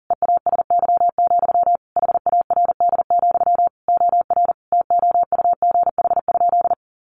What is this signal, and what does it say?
Morse code, heard as a keyed tone, about 40 wpm, 700 Hz. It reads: EASY, HARD, OR TOUGH?
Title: Sentences from Top 1200 Words 40wpm